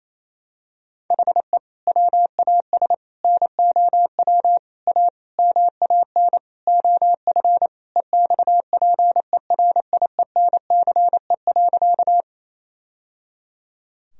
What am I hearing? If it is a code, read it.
Morse code, heard as a keyed tone, about 28 wpm, 700 Hz. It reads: HE WAS NOW A MAN OF EXPERIENCE.
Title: Sentences from Top 1000 Words - Review Part 3 - 28wpm